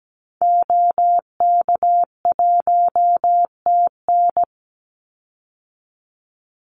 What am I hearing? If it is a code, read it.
Morse code, heard as a keyed tone, about 17 wpm, 700 Hz. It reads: OK1TN
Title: Call Signs Repeated 2x - 17wpm